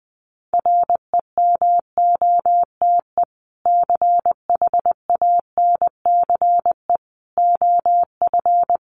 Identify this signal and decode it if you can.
Morse code, heard as a keyed tone, about 19 wpm, 700 Hz. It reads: REMOTE CHANCE OF
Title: Sentences from Top 1300 Words 20wpm